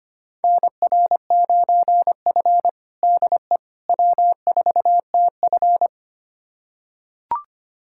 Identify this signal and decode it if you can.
Morse code, heard as a keyed tone, about 25 wpm, 700 Hz. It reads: NR9F DE W4TF
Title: USA Callsign DE Callsign - 4 Characters - 25wpm